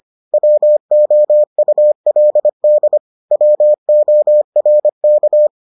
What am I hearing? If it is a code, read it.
Morse code, heard as a keyed tone, about 25 wpm, 600 Hz. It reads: WOULD WORK